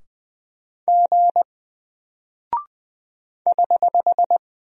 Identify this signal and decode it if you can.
Morse code, heard as a keyed tone, about 20 wpm, 700 Hz. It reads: G <HH>
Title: Single Character (Letter:Number:Punctuation:Prosign) Alt 20wpm